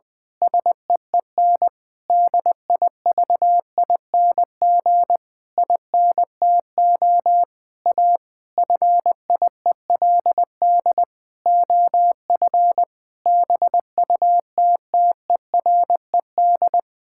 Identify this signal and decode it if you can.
Morse code, heard as a keyed tone, about 20 wpm, 700 Hz. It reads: SEEN DIVING INTO A FIELD OF BUTTERED
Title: Pangram Challenge v2 - 20wpm